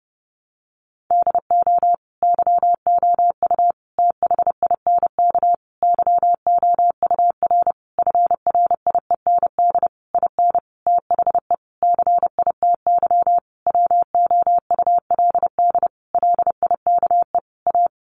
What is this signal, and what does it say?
Morse code, heard as a keyed tone, about 30 wpm, 700 Hz. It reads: DO YOU THINK YOUR FRIEND IN THE CITY WOULD LIKE A
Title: Sentences from Top 1100 Words 30wpm